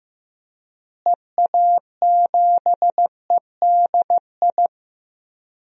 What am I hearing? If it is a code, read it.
Morse code, heard as a keyed tone, about 15 wpm, 700 Hz. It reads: EA7EDI
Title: DXCC-100 Call Signs - 6 Characters - Repeated 2x - 15wpm